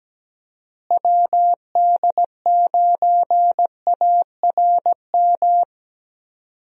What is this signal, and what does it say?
Morse code, heard as a keyed tone, about 17 wpm, 700 Hz. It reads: WD9ARM
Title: USA Call Signs - 6 Characters - Repeated 2x - 17wpm